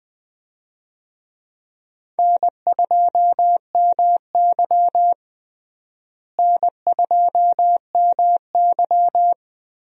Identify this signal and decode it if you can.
Morse code, heard as a keyed tone, about 20 wpm, 700 Hz. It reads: N2MY N2MY
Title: USA Call Signs - 4 Characters - Repeated 2x - 20wpm